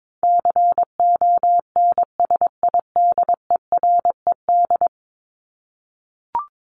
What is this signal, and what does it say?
Morse code, heard as a keyed tone, about 22 wpm, 700 Hz. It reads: CONSIDERED
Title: ED Suffix - Top 100 Words with Base Words 22wpm